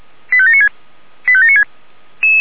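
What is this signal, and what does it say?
Five-tone selective call (ZVEI), heard as quick runs of five tones, tones of about 70 ms. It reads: 85796 85796
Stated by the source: ZVEI five-tone selective call received by SDR.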